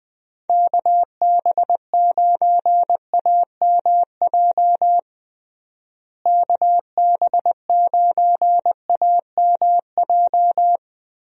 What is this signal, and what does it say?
Morse code, heard as a keyed tone, about 21 wpm, 700 Hz. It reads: KB9AMJ KB9AMJ
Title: USA Call Signs - 6 Characters - Repeated 3x - 20wpm